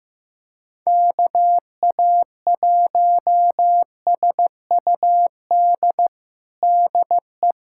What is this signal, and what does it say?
Morse code, heard as a keyed tone, about 16 wpm, 700 Hz. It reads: KA1SUD DE
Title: USA Callsign DE Callsign - 6 Characters - 15wpm